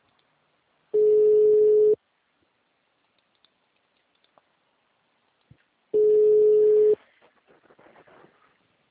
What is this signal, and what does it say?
Ringback tone (Europe, 425 Hz).